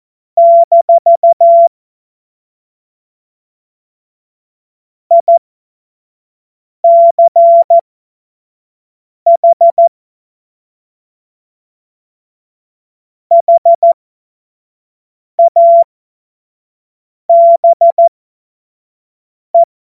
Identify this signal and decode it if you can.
Morse code, heard as a keyed tone, about 14 wpm, spaced as for 5 wpm, 680 Hz. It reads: - ICH HABE
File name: Morsekurs / Witze / Witz 26 März 2024 25 F70